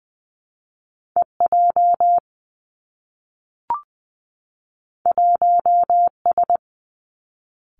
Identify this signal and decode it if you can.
Morse code, heard as a keyed tone, about 20 wpm, 700 Hz. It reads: E J 1 S
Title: Two Challenging Letter-Number Combinations 20wpm